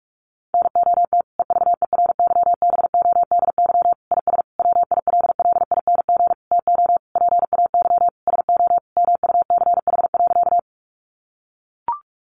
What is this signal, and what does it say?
Morse code, heard as a keyed tone, about 44 wpm, 700 Hz. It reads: NOT EVERYBODY IS WILLING TO PAY SO MUCH.
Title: Sentences from Top 1400 Words 45wpm